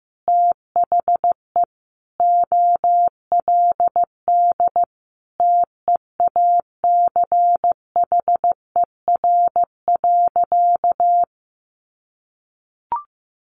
Morse code, heard as a keyed tone, about 15 wpm, 700 Hz. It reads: THE OLD TEACHER.
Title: Sentences from Top 900 Words 15wpm